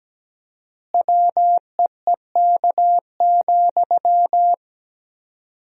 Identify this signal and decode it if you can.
Morse code, heard as a keyed tone, about 17 wpm, 700 Hz. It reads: WEEK,
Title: 4x Word-Spacing - Sentences from Top 500 Words - 17wpm